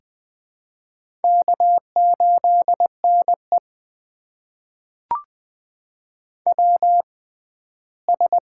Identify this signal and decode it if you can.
Morse code, heard as a keyed tone, about 20 wpm, spaced as for 14 wpm, 700 Hz. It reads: K8NE WS
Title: USA Call Signs - 4 Characters - 6x Character Spacing - 20wpm